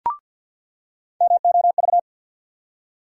Morse code, heard as a keyed tone, about 53 wpm, 700 Hz. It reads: MOV
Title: USA Call Sign Postfixes - 3 Letters 50wpm